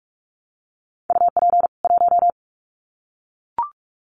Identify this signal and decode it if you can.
Morse code, heard as a keyed tone, about 44 wpm, 700 Hz. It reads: UP 1
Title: Common QSO Phrases 45wpm